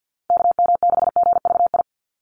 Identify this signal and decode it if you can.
Morse code, heard as a keyed tone, about 50 wpm, 700 Hz. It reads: KM6GVI
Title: USA Call Signs 50wpm